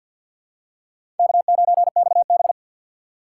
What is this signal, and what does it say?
Morse code, heard as a keyed tone, about 53 wpm, 690 Hz. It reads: K9XB
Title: USA Call Signs - 2x Character Spacing - 50wpm